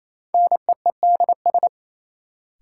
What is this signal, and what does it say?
Morse code, heard as a keyed tone, about 28 wpm, 700 Hz. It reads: NEEDS
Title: Sets of 3 Words - New Word Drill - 28wpm